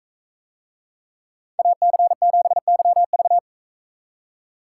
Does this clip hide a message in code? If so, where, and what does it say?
Yes: Morse code from 1.6 to 3.4 s, heard as a keyed tone, about 42 wpm, 700 Hz. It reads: AC7YV